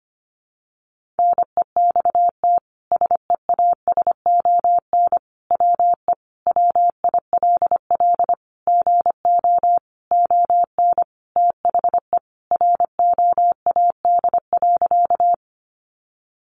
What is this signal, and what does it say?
Morse code, heard as a keyed tone, about 25 wpm, 700 Hz. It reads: NEXT SEASON WE WILL GO ON THE ROAD.
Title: Sentences from Top 1400 Words 25wpm